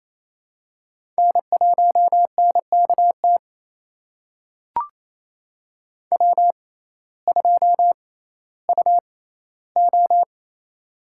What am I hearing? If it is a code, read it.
Morse code, heard as a keyed tone, about 28 wpm, spaced as for 15 wpm, 700 Hz. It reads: N1NKT W2UO
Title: USA Call Signs - 6x Character Spacing - 28wpm